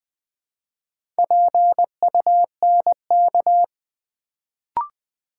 Morse code, heard as a keyed tone, about 21 wpm, 700 Hz. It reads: PUNK
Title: Build A Word - 1 Letter At A Time - 4-Letter Words - 20wpm